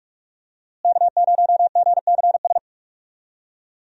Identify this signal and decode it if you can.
Morse code, heard as a keyed tone, about 45 wpm, 700 Hz. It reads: K0GCS
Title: USA Call Signs - 5 Characters - Repeated 2x - 45wpm